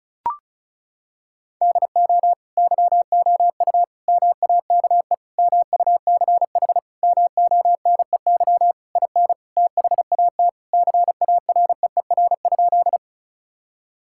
Morse code, heard as a keyed tone, about 36 wpm, 700 Hz. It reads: DO YOU MAKE MUCH MONEY IN THAT CAREER?
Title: Sentences from Top 1000 Words - Review Part 6 - 35wpm